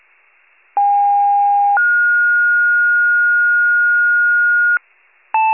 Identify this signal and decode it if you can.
Two-tone page, heard as a one-second tone, then a three-second tone, about 810 then 1430 Hz, with noise. It is heard from 0.8 to 4.8 s.
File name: wcalert